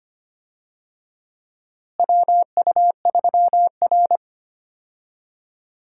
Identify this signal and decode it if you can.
Morse code, heard as a keyed tone, about 25 wpm, 700 Hz. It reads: WU3R
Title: USA Call Signs - 4 Characters - Repeated 2x - 25wpm